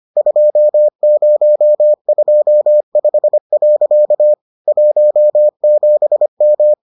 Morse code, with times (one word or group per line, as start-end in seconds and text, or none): 0.2-4.3 2025.
4.7-6.7 17M